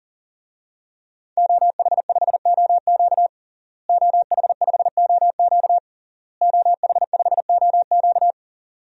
Morse code, heard as a keyed tone, about 40 wpm, 700 Hz. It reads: OH5OQ OH5OQ OH5OQ
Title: DXCC-100 Call Signs - 5 Characters - Repeated 3x - 40wpm